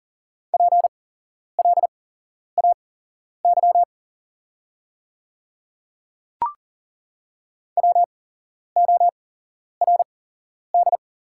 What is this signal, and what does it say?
Morse code, heard as a keyed tone, about 40 wpm, spaced as for 11 wpm, 690 Hz. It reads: PLAY WORD
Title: 8x Character Spacing - Top 200 Words - 40wpm